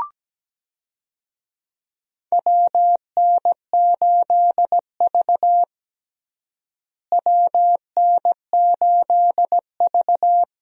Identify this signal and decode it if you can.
Morse code, heard as a keyed tone, about 17 wpm, 700 Hz. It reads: WN8V WN8V
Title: USA Call Signs - 4 Characters - Repeated 2x - 17wpm